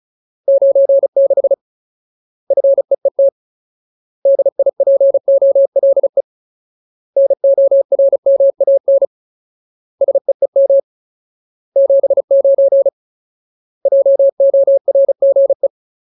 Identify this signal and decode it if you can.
Morse code, heard as a keyed tone, about 34 wpm, 550 Hz. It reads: 96 FEET DIPOLE NORMAN SEEM 79 JORGE
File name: Ultimate Word List / 35WPM Ultimate